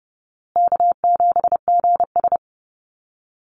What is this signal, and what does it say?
Morse code, heard as a keyed tone, about 28 wpm, 700 Hz. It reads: K7GS
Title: Easy Call Signs Repeated 3x - 30wpm